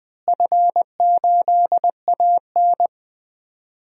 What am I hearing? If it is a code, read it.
Morse code, heard as a keyed tone, about 20 wpm, 700 Hz. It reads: F8AN
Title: DXCC-100 Call Signs - 4 Characters - Repeated 2x - 20wpm